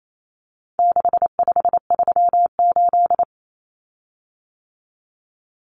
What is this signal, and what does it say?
Morse code, heard as a keyed tone, about 28 wpm, 700 Hz. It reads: 6538
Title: Common Cut Numbers - 4-Digit 28wpm